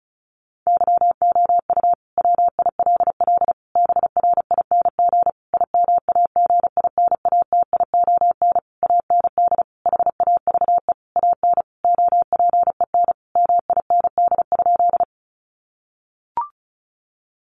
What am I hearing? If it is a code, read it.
Morse code, heard as a keyed tone, about 34 wpm, 700 Hz. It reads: YOU WILL BRING IMAGINATION AND HAVE AN OPEN MIND?
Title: Sentences from Top 1300 Words 35wpm